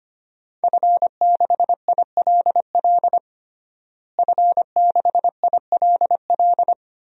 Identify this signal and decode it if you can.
Morse code, heard as a keyed tone, about 25 wpm, 700 Hz. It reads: F6ILL F6ILL
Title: DXCC-100 Call Signs - 5 Characters - Repeated 2x - 25wpm